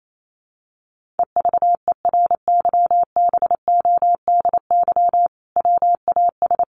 Morse code, heard as a keyed tone, about 28 wpm, 700 Hz. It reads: EVERYBODY WAS
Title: Sentences from Top 1400 Words 28wpm